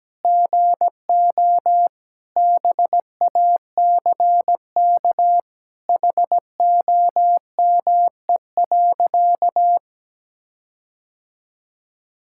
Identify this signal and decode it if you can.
Morse code, heard as a keyed tone, about 17 wpm, 700 Hz. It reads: GO BACK HOME.
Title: Sentences from Top 1600 Words 17wpm